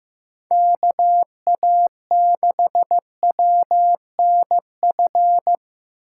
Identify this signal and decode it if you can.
Morse code, heard as a keyed tone, about 18 wpm, 700 Hz. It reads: KA6WNF
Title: Speed Racing - USA 2x3 Call Signs 15wpm